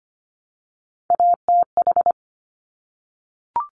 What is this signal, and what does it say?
Morse code, heard as a keyed tone, about 25 wpm, 700 Hz. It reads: ATH
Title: Top Three Letter Combinations in Words 25wpm